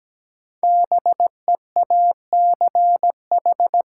Morse code, heard as a keyed tone, about 17 wpm, 700 Hz. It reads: BEACH
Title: Sentences from Top 1500 Words - New Word Drill - 17wpm